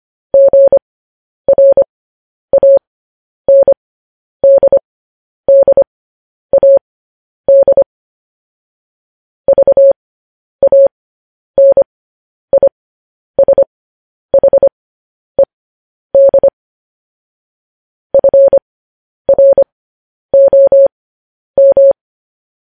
Morse code, heard as a keyed tone, about 25 wpm, spaced as for 10 wpm, 550 Hz. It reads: GRANDDAD VANISHED FROM
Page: news-headlines-in-morse-code-at-10-wpm